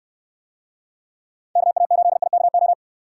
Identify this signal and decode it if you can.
Morse code, heard as a keyed tone, about 65 wpm, 700 Hz. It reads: KA8IKO